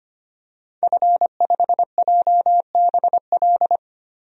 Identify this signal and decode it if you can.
Morse code, heard as a keyed tone, about 25 wpm, 700 Hz. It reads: F5JBL
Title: DXCC-100 Call Signs - 5 Characters - 25wpm